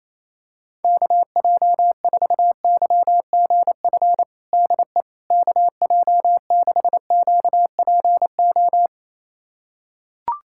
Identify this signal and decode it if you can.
Morse code, heard as a keyed tone, about 27 wpm, 700 Hz. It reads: KJ4YGF DE KJ6QPO
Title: USA Callsign DE Callsign 28wpm